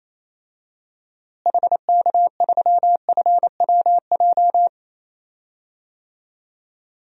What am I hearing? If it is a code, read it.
Morse code, heard as a keyed tone, about 28 wpm, 700 Hz. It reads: HK3FWJ
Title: DXCC-100 Call Signs - 6 Characters - Repeated 2x - 28wpm